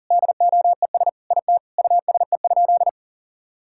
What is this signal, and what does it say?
Morse code, heard as a keyed tone, about 40 wpm, 700 Hz. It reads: DOES IT USE?
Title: Sentences from Top 1000 Words - Review Part 6 - 40wpm